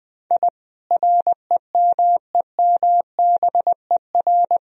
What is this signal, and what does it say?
Morse code, heard as a keyed tone, about 20 wpm, 700 Hz. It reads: I REMEMBER
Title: Sentences from Top 1000 Words - Review Part 1 - 20wpm